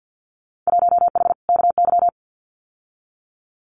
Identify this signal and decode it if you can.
Morse code, heard as a keyed tone, about 50 wpm, 700 Hz. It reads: 1H KY
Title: Winter Field Day Exchanges 50wpm